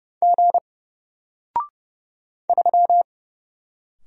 Morse code, heard as a keyed tone, about 30 wpm, 700 Hz. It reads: G 3
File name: Single-Letter-Number-30wpm